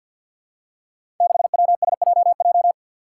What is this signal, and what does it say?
Morse code, heard as a keyed tone, about 50 wpm, 700 Hz. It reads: 6WSJJ
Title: Five Challenging Letter-Number Combinations 50wpm